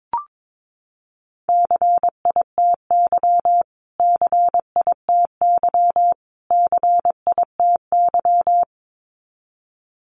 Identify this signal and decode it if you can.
Morse code, heard as a keyed tone, about 22 wpm, 700 Hz. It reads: CITY CITY CITY
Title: Top 500 Words Repeated 22wpm